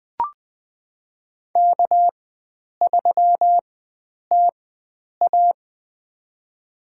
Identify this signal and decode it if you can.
Morse code, heard as a keyed tone, about 20 wpm, spaced as for 9 wpm, 700 Hz. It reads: K3TA
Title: USA Call Signs - 4x Character Spacing - 20wpm